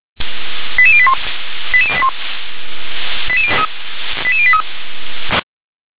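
Five-tone selective call (ZVEI), heard as quick runs of five tones, tones of about 70 ms. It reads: 88081 88081 88083
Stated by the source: Received on an SDR; ZVEI five-tone call 88101.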